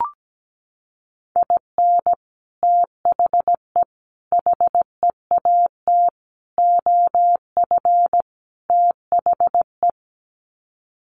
Morse code, heard as a keyed tone, about 17 wpm, 700 Hz. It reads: IN THE HEAT OF THE
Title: Sets of 5 Words Encore2 17wpm